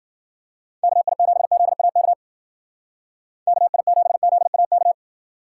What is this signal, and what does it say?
Morse code, heard as a keyed tone, about 50 wpm, 700 Hz. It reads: KI7ZAK KI7ZAK
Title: Speed Racing - USA 2x3 Call Signs 40wpm